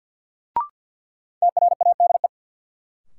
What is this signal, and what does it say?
Morse code, heard as a keyed tone, about 50 wpm, 700 Hz. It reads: TRADE
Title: Sets of 5 Words Encore - New Word Drill - 50wpm